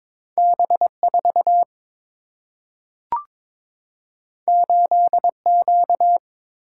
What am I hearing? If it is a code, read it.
Morse code, heard as a keyed tone, about 22 wpm, 700 Hz. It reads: B 4 8 Q
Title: Single-Letter Single-Number 22wpm